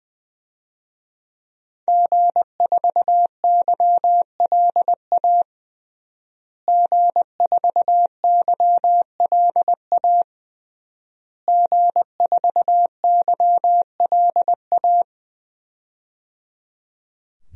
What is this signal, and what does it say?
Morse code, heard as a keyed tone, about 20 wpm, 700 Hz. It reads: G4YLA G4YLA G4YLA
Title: DXCC-100 Call Signs - 5 Characters - Repeated 3x - 20wpm